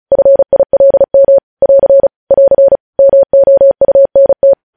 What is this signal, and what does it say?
Morse code, heard as a keyed tone, about 36 wpm, 550 Hz. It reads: FILM <AR> <AR> MOUNT
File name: podcast / News headlines-1x-35x35